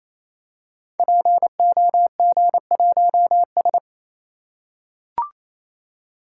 Morse code, heard as a keyed tone, about 28 wpm, 700 Hz. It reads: POG1S
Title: Code Groups - 5 Letter:Numbers - 28wpm